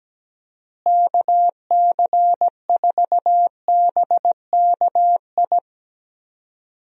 Morse code, heard as a keyed tone, about 17 wpm, 700 Hz. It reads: KC4BKI
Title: USA Call Signs - 6 Characters - Repeated 3x - 17wpm